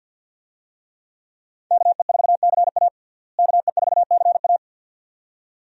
Morse code, heard as a keyed tone, about 52 wpm, 700 Hz. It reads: KE4CA KE4CA
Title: USA Call Signs - 5 Characters - Repeated 2x - 50wpm